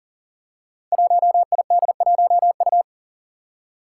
Morse code, heard as a keyed tone, about 40 wpm, 700 Hz. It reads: 1ID1U
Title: Five Challenging Letter-Number Combinations 40wpm